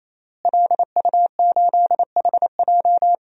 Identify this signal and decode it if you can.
Morse code, heard as a keyed tone, about 28 wpm, 700 Hz. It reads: LU8HJ
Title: DXCC-100 Call Signs - 5 Characters - Repeated 3x - 28wpm